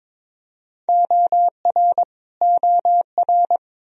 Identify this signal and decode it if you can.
Morse code, heard as a keyed tone, about 22 wpm, 700 Hz. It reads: OR OR
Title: Top 100 Words Repeated 22wpm